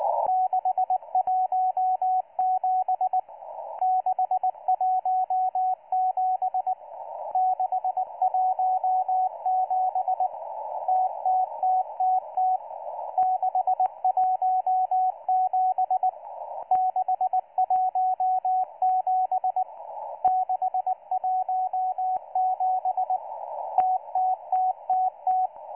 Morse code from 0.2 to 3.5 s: 617